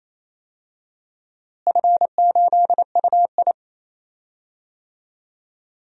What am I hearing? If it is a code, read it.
Morse code, heard as a keyed tone, about 28 wpm, 700 Hz. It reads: F8UI
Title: DXCC-100 Call Signs - 4 Characters - Repeated 2x - 28wpm